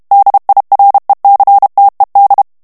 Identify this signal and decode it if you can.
Morse code, heard as a keyed tone, about 32 wpm, 800 Hz. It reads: DIRECTED
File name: morse mp3 / ANGLAIS texte clair 3 vitesse 30 mots minute